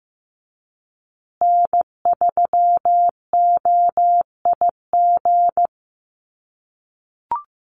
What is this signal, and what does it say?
Morse code, heard as a keyed tone, about 15 wpm, 700 Hz. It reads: N3OIG
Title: USA Call Signs Repeated 2x 15wpm